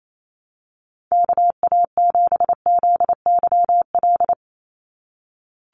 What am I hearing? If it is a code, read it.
Morse code, heard as a keyed tone, about 28 wpm, 700 Hz. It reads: KA7ZYL
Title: USA Call Signs 28wpm